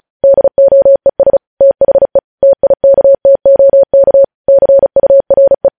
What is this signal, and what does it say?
Morse code, heard as a keyed tone, about 36 wpm, 550 Hz. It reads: DOES THE TIKTOK CURE
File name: podcast / News headlines-1x-35x35